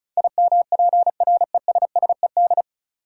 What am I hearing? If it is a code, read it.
Morse code, heard as a keyed tone, about 36 wpm, 700 Hz. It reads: IMPRESSED
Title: Sentences from Top 1600 Words - New Word Drill - 35wpm